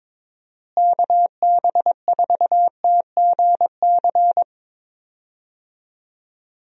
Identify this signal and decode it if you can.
Morse code, heard as a keyed tone, about 22 wpm, 700 Hz. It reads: KB4TGC
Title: USA Call Signs - 6 Characters - Repeated 2x - 22wpm